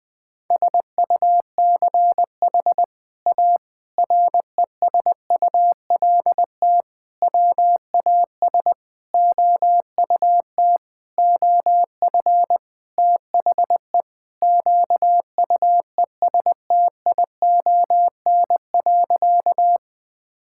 Morse code, heard as a keyed tone, about 20 wpm, 700 Hz. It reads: SUCH A RESULT WAS OUT OF THE QUESTION.
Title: Sentences from Top 1000 Words - Review Part 5 - 20wpm